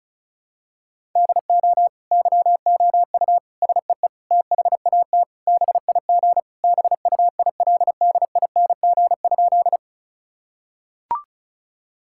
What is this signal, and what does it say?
Morse code, heard as a keyed tone, about 34 wpm, 700 Hz. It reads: DO YOU SEE THAT BIG BUILDING?
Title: Sentences from Top 1000 Words - Review Part 5 - 35wpm